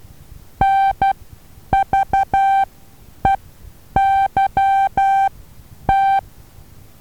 Morse code, heard as a keyed tone, about 12 wpm, spaced as for 9 wpm, 780 Hz. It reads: NVEYT
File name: Taviro HA6VA B